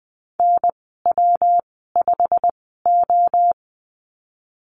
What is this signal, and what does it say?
Morse code, heard as a keyed tone, about 20 wpm, spaced as for 15 wpm, 700 Hz. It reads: NW5O
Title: 128 Lesson 21g - Number 6 - Call Signs 20wpm